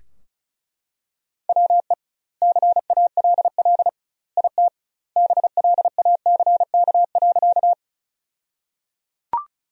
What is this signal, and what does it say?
Morse code, heard as a keyed tone, about 34 wpm, 700 Hz. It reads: WE CALL IT BLACK.
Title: 2x Word-Spacing - Sentences from Top 400 Words - 35wpm